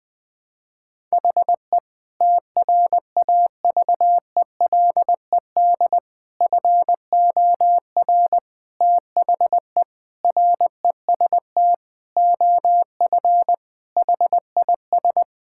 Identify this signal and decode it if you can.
Morse code, heard as a keyed tone, about 20 wpm, 700 Hz. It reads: HE TRAVELED FOR THE REST OF HIS
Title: Sentences with 6-Word Phrases - Spoken Beginning - 20wpm